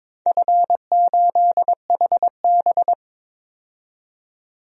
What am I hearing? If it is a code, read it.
Morse code, heard as a keyed tone, about 22 wpm, 700 Hz. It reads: F8HB
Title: DXCC-100 Call Signs - 4 Characters - 22wpm